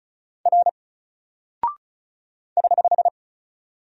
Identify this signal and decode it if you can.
Morse code, heard as a keyed tone, about 35 wpm, 700 Hz. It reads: R <HH>
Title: Single Character (Letter:Number:Punctuation:Prosign) 35wpm